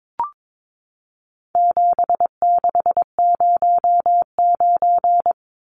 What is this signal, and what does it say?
Morse code, heard as a keyed tone, about 22 wpm, 700 Hz. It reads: 7609
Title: Numbers - 4-Digit 22wpm